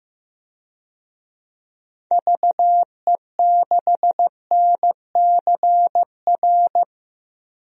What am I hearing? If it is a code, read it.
Morse code, heard as a keyed tone, about 15 wpm, 700 Hz. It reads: VE6NCR
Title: DXCC-100 Call Signs - 6 Characters - Repeated 3x - 15wpm